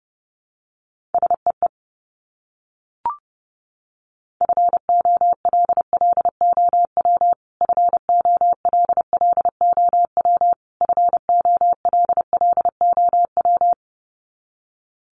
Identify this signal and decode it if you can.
Morse code, heard as a keyed tone, about 30 wpm, 700 Hz. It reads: SEE FOLLOW FOLLOW FOLLOW
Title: Top 500 Words Repeated 30wpm